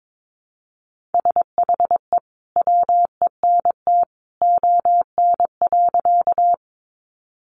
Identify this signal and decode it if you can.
Morse code, heard as a keyed tone, about 22 wpm, 700 Hz. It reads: SHE WENT ON.
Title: Sentences from Top 200 Words 22wpm